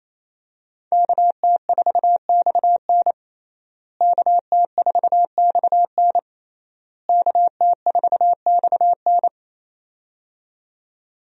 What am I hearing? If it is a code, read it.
Morse code, heard as a keyed tone, about 28 wpm, 700 Hz. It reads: KT4XN KT4XN KT4XN
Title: USA Call Signs - 5 Characters - Repeated 3x - 28wpm